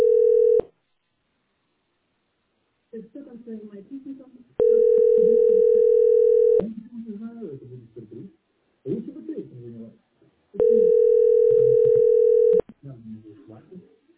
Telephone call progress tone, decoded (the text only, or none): ringback tone